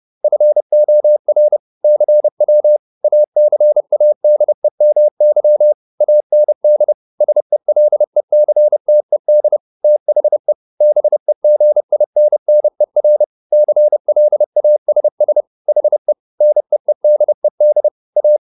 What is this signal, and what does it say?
Morse code, heard as a keyed tone, about 30 wpm, 600 Hz. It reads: FOR CW ACADEMY AND SELECTED THE BEGINNER CLASS HE NEEDED A
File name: SS301 30